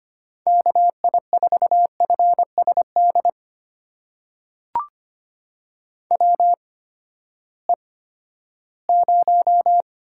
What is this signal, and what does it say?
Morse code, heard as a keyed tone, about 25 wpm, spaced as for 14 wpm, 700 Hz. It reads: KI4FSD WE0
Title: USA Call Signs - 8x Character Spacing - 25wpm